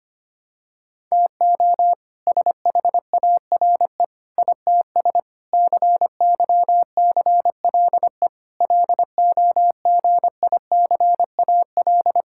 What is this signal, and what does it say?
Morse code, heard as a keyed tone, about 25 wpm, 700 Hz. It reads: TO SHARE ITS CYCLE LOGICAL
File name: practice / 25wpm / All-Age Appropriate Jokes - 25wpm